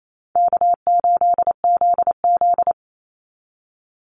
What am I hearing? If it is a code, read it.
Morse code, heard as a keyed tone, about 28 wpm, 700 Hz. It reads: K8ZZ
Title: Call Signs Repeated 3x - 28wpm